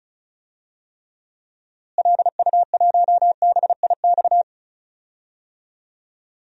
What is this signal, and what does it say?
Morse code, heard as a keyed tone, about 36 wpm, 700 Hz. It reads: LU1BIX
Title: DXCC-100 Call Signs - 6 Characters - Repeated 3x - 35wpm